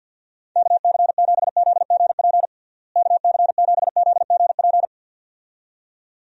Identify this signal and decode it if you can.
Morse code, heard as a keyed tone, about 49 wpm, 700 Hz. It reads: KC7ZGP KC7ZGP
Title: USA Call Signs - 6 Characters - Repeated 2x - 50wpm